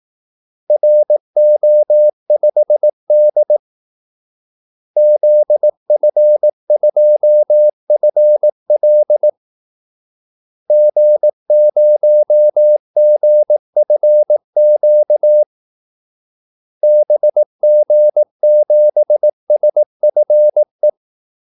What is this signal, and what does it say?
Morse code, heard as a keyed tone, about 18 wpm, 600 Hz. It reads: RO5D ZF2FL G0GFQ BG7SFE
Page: Kaldesignaler 18 wpm